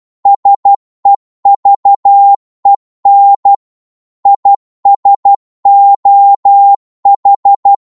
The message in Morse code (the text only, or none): SEVEN IS OH